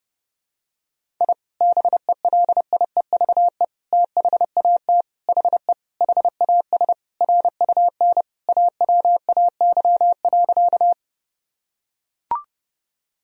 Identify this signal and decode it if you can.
Morse code, heard as a keyed tone, about 30 wpm, 700 Hz. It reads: I BELIEVE THAT HE HAS RUN AWAY.
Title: Sentences from Top 1000 Words - Review Part 1 - 30wpm